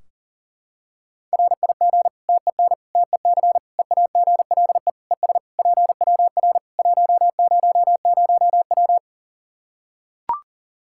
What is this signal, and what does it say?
Morse code, heard as a keyed tone, about 37 wpm, 700 Hz. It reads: RIG TEN TEC EAGLE ES PWR 100W
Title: Practice QSO - Exchange 2 - Info - 40wpm